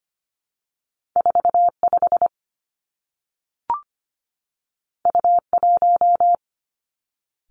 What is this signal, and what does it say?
Morse code, heard as a keyed tone, about 25 wpm, 700 Hz. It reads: 4 5 U 1
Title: Two Challenging Letter-Number Combinations 25wpm